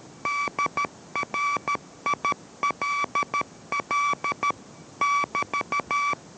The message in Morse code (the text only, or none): DRILL <BT>